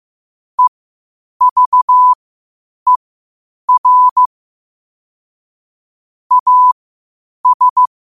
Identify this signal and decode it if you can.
Morse code, heard as a keyed tone, about 15 wpm, spaced as for 8 wpm, 1000 Hz. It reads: EVER AS
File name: qotdfiles / 8wpm